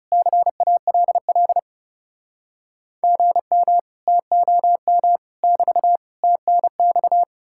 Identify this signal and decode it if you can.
Morse code, heard as a keyed tone, about 40 wpm, 700 Hz. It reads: CALL GM TOM <BT> TNX